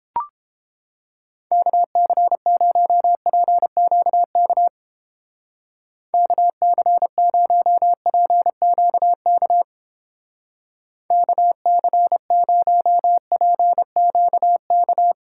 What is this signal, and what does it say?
Morse code, heard as a keyed tone, about 29 wpm, 700 Hz. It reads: KC0PQK KC0PQK KC0PQK